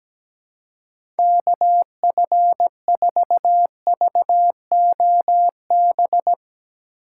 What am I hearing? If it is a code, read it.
Morse code, heard as a keyed tone, about 17 wpm, 700 Hz. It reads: KF4VOB
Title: USA Call Signs - 4x Character Spacing - 17wpm